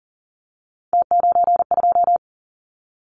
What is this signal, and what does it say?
Morse code, heard as a keyed tone, about 40 wpm, 700 Hz. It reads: T92
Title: Callsign Prefixes for Top 50 DXCC - 40wpm